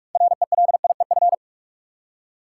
Morse code, heard as a keyed tone, about 46 wpm, 700 Hz. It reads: RELIEF
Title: Six Letter Words - 45wpm